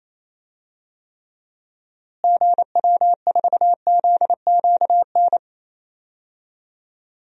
Morse code, heard as a keyed tone, about 27 wpm, 700 Hz. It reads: GW4ZQN
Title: DXCC-100 Call Signs - 6 Characters - Repeated 3x - 28wpm